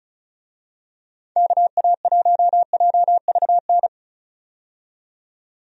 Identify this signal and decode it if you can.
Morse code, heard as a keyed tone, about 36 wpm, 700 Hz. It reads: KA1JVN
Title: USA Call Signs - 6 Characters - 35wpm